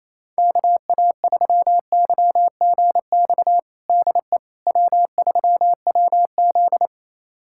Morse code, heard as a keyed tone, about 28 wpm, 700 Hz. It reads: KA3YGX DE W3WZ
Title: USA Callsign DE Callsign 28wpm